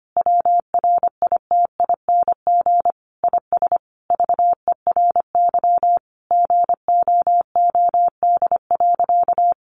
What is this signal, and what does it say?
Morse code, heard as a keyed tone, about 25 wpm, 700 Hz. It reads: WRITING IS VERY GOOD.
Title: Sentences from Top 800 Words 25wpm